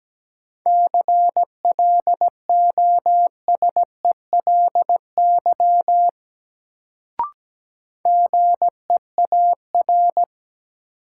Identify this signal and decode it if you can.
Morse code, heard as a keyed tone, about 17 wpm, 700 Hz. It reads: CLOSELY GEAR
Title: Sentences from Top 1400 Words - New Word Drill Alt - 17wpm